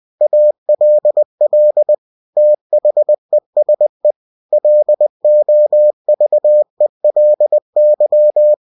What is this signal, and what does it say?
Morse code, heard as a keyed tone, about 20 wpm, 600 Hz. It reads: ALL THESE LOVELY
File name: cw-books / winnie-the-pooh / winnie-20wpm-page-0109